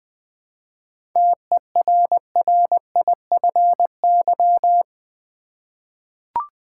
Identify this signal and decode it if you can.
Morse code, heard as a keyed tone, about 20 wpm, 700 Hz. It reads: TERRIFY
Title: 7 Letter Words - First 3 Letters Given - 20wpm